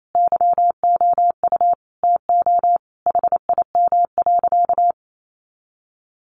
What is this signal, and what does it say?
Morse code, heard as a keyed tone, about 28 wpm, 700 Hz. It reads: YOU TO HIM.
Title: Sentences from Top 1200 Words 28wpm